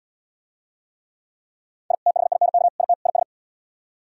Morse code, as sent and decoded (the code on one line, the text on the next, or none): .. / - .... .. -. -.- / .. - / .. ...
I THINK IT IS